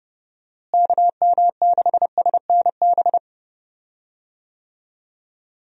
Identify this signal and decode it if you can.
Morse code, heard as a keyed tone, about 30 wpm, 700 Hz. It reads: KM6SNB
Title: USA Call Signs - 6 Characters - Repeated 2x - 30wpm